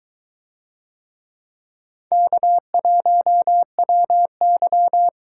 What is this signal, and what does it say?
Morse code, heard as a keyed tone, about 22 wpm, 700 Hz. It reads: K1WY